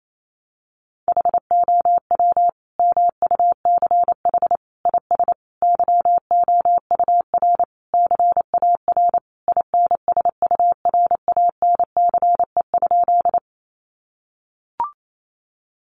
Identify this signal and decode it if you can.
Morse code, heard as a keyed tone, about 28 wpm, 700 Hz. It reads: HOW MUCH IS YOUR CAR INSURANCE?
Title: Sentences from Top 1100 Words 28wpm